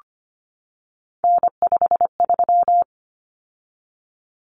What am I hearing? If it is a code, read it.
Morse code, heard as a keyed tone, about 25 wpm, 700 Hz. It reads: N53
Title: Common Cut Numbers - 3-Digit 25wpm